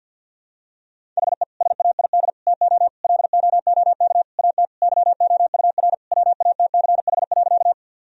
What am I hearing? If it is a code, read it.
Morse code, heard as a keyed tone, about 50 wpm, 700 Hz. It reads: HE SAID TO LOOK AT YOUR WATCH.
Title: Sentences from Top 1000 Words - Review Part 1 - 50wpm